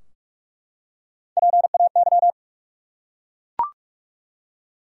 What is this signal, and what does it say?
Morse code, heard as a keyed tone, about 41 wpm, 700 Hz. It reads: PAY
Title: Three Letter Words - 45wpm